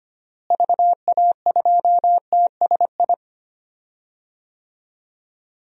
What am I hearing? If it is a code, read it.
Morse code, heard as a keyed tone, about 25 wpm, 700 Hz. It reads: VA2TSI
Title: DXCC-100 Call Signs - 6 Characters - Repeated 2x - 25wpm